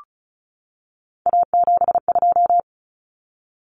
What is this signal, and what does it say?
Morse code, heard as a keyed tone, about 34 wpm, 700 Hz. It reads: A72
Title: Common Cut Numbers - 3-Digit 35wpm